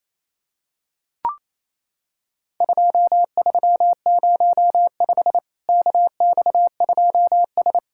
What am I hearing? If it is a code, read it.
Morse code, heard as a keyed tone, about 28 wpm, 700 Hz. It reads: 2305 KX2S
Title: Worked All Europe DX Contest - QTC Exchange - 28wpm